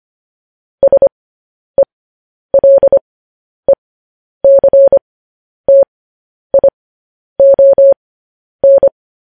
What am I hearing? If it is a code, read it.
Morse code, heard as a keyed tone, about 25 wpm, spaced as for 10 wpm, 550 Hz. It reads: SELECTION